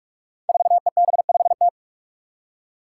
Morse code, heard as a keyed tone, about 46 wpm, 700 Hz. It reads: 4EB5T
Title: Five Challenging Letter-Number Combinations 45wpm